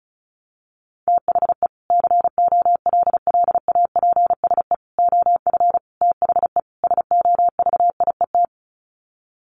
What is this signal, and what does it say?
Morse code, heard as a keyed tone, about 34 wpm, 700 Hz. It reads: THE COLLAPSE OF THE SOVIET
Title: Sets of 5 Words Encore 35wpm